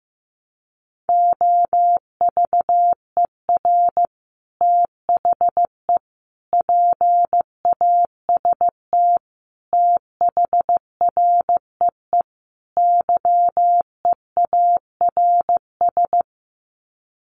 Morse code, heard as a keyed tone, about 15 wpm, 700 Hz. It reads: OVER THE PAST THREE YEARS
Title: Sets of 5 Words Encore2 15wpm